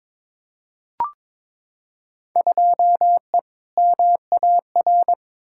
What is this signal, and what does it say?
Morse code, heard as a keyed tone, about 22 wpm, 700 Hz. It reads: 2E MAR
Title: Field Day Exchanges Encore - 22wpm